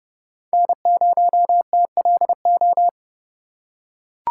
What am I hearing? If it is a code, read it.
Morse code, heard as a keyed tone, about 30 wpm, 690 Hz. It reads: N0TLO
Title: USA Call Signs - 5 Characters - 2x Character Spacing - 30wpm